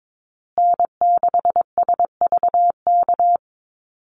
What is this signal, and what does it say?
Morse code, heard as a keyed tone, about 22 wpm, 700 Hz. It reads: N6SVK
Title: USA Call Signs Repeated 2x 22wpm